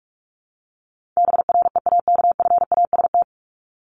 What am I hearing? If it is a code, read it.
Morse code, heard as a keyed tone, about 44 wpm, 700 Hz. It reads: BREAKFAST
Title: Top 1251-1500 Words 45wpm